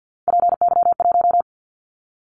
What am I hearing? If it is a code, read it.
Morse code, heard as a keyed tone, about 50 wpm, 700 Hz. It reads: PY1
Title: Callsign Prefixes for Top 50 DXCC - 50wpm